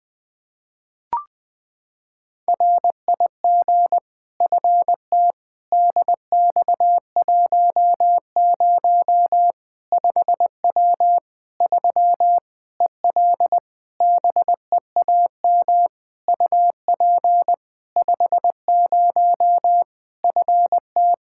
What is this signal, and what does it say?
Morse code, heard as a keyed tone, about 20 wpm, 700 Hz. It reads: RIG FT DX10 5W 3 EL BEAM UP 50 FT
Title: Practice QSO - Exchange 2 - Info - 20wpm